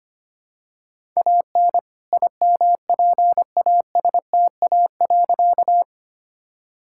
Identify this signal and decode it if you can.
Morse code, heard as a keyed tone, about 25 wpm, 700 Hz. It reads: AN IMPASTA.
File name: practice / 25wpm / All-Age Appropriate Jokes - 25wpm